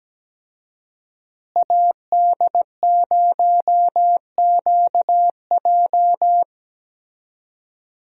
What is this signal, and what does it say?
Morse code, heard as a keyed tone, about 17 wpm, 700 Hz. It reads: AD0QJ
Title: USA Call Signs - 5 Characters - Repeated 2x - 17wpm